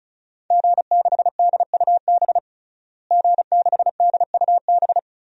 Morse code, heard as a keyed tone, about 34 wpm, 700 Hz. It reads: G6DUB G6DUB
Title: DXCC-100 Call Signs - 5 Characters - Repeated 2x - 35wpm